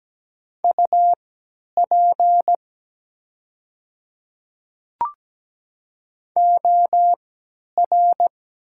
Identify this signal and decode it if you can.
Morse code, heard as a keyed tone, about 17 wpm, spaced as for 10 wpm, 700 Hz. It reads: UP OR
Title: 3x Character Spacing - Top 300 Words - 17wpm